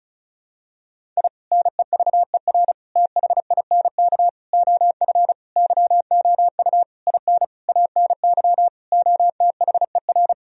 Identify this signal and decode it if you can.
Morse code, heard as a keyed tone, about 34 wpm, 700 Hz. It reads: I NEVER THINK OF YOU IN ANY OTHER
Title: Sentences from Top 1000 Words - Review Part 1 - 35wpm